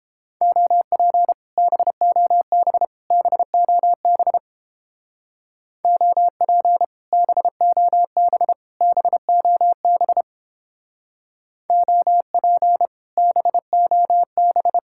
Morse code, heard as a keyed tone, about 30 wpm, 700 Hz. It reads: OP BOB BOB OP BOB BOB OP BOB